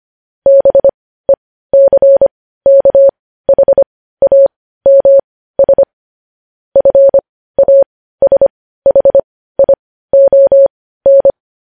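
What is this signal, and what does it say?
Morse code, heard as a keyed tone, about 25 wpm, spaced as for 15 wpm, 550 Hz. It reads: BECKHAMS FASHION
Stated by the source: Morse practice sending news headlines at 15 wpm.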